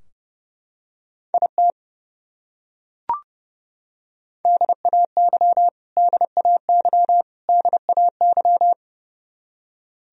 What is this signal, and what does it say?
Morse code, heard as a keyed tone, about 30 wpm, 700 Hz. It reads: IT DAY DAY DAY
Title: Top 100 Words Repeated 30wpm